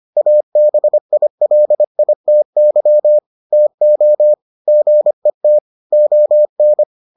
Morse code, heard as a keyed tone, about 25 wpm, 600 Hz. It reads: ABILITY TO GET ON